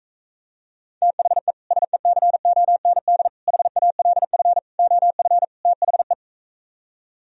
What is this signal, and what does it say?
Morse code, heard as a keyed tone, about 45 wpm, 700 Hz. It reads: THE SECOND HALF OF THE